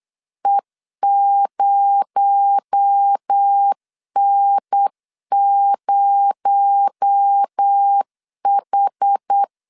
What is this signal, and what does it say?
Morse code, heard as a keyed tone, about 8 wpm, 800 Hz. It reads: E0N0H
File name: LEZIONI / 6 PKX0 / 6 PKX0 8wpm 3341